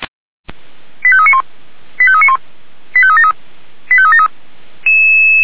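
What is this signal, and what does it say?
Five-tone selective call (ZVEI), heard as quick runs of five tones, tones of about 70 ms. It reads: 85391 85391 85373 85373
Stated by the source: SDR-received ZVEI five-tone call 85719.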